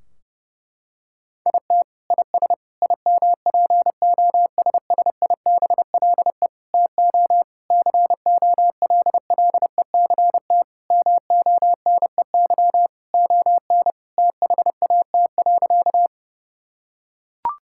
Morse code, heard as a keyed tone, about 30 wpm, 700 Hz. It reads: IT IS IMPOSSIBLE TO COLLECT MONEY ON THAT.
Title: Sentences from Top 1600 Words 30wpm